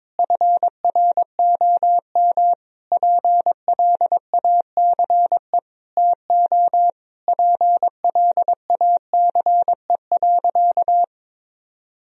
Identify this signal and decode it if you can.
Morse code, heard as a keyed tone, about 21 wpm, 700 Hz. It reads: FROM PLACE TO PLACE.
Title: Sentences from Top 1000 Words - Review Part 1 - 22wpm